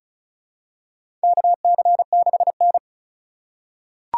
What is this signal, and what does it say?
Morse code, heard as a keyed tone, about 36 wpm, 700 Hz. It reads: KC6N
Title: USA Call Signs - 4 Characters - 2x Character Spacing - 35wpm